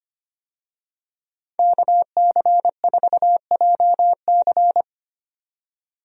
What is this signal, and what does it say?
Morse code, heard as a keyed tone, about 25 wpm, 700 Hz. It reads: KC4JC
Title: USA Call Signs - 5 Characters - 25wpm